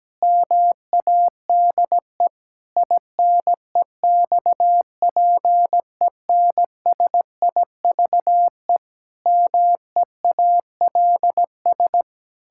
Morse code, heard as a keyed tone, about 17 wpm, 700 Hz. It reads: MADE INEXPENSIVE MEALS
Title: Pangram Challenge v2 - 17wpm